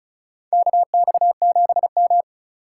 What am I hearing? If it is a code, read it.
Morse code, heard as a keyed tone, about 36 wpm, 700 Hz. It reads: KX7M
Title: K1USN SST QSO Practice - 35wpm